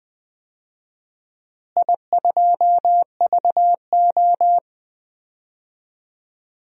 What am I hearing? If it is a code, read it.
Morse code, heard as a keyed tone, about 21 wpm, 700 Hz. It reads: I2VO
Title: DXCC-100 Call Signs - 4 Characters - Repeated 2x - 20wpm